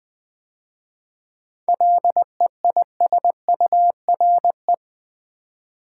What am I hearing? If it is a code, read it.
Morse code, heard as a keyed tone, about 20 wpm, 690 Hz. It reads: LEISURE
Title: 7 Letter Words - First 3 Letters Given - 20wpm